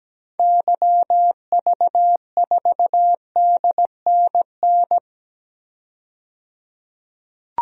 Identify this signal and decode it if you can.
Morse code, heard as a keyed tone, about 17 wpm, 700 Hz. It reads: YV4DNN
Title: DXCC-100 Call Signs - 6 Characters - 17wpm